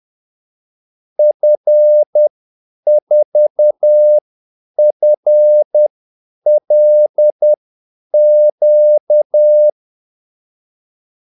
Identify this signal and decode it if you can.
Morse code, heard as a keyed tone, about 10 wpm, spaced as for 8 wpm, 600 Hz. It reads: F4FLQ
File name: Callsigns-10-8wpm